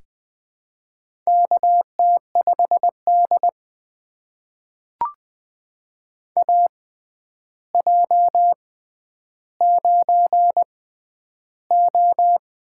Morse code, heard as a keyed tone, about 20 wpm, spaced as for 10 wpm, 700 Hz. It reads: KT5D AJ9O
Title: USA Call Signs - 4 Characters - 6x Character Spacing - 20wpm